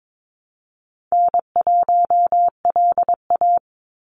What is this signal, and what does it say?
Morse code, heard as a keyed tone, about 22 wpm, 700 Hz. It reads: N1LA
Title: USA Call Signs 22wpm